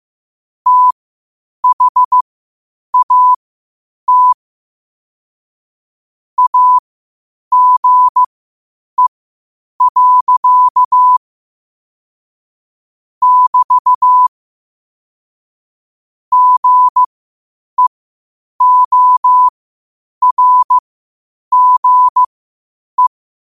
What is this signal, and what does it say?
Morse code, heard as a keyed tone, about 15 wpm, spaced as for 8 wpm, 1000 Hz. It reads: THAT AGE. <BT> GEORGE
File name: qotdfiles / 8wpm